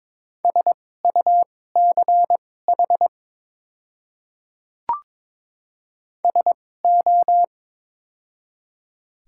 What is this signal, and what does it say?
Morse code, heard as a keyed tone, about 22 wpm, spaced as for 16 wpm, 700 Hz. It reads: SUCH SO
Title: 2x Character Spacing - Top 200 Words - 22wpm